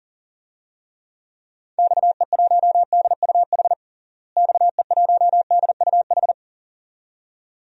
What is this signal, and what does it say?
Morse code, heard as a keyed tone, about 40 wpm, 700 Hz. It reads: XE1DUH XE1DUH
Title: DXCC-100 Call Signs - 6 Characters - Repeated 2x - 40wpm